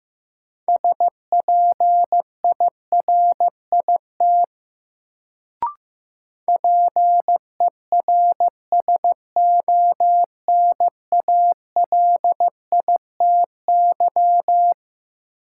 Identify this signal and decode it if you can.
Morse code, heard as a keyed tone, about 15 wpm, 700 Hz. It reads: SPIRIT PERSONALITY
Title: Sentences from Top 1400 Words - New Word Drill Alt - 15wpm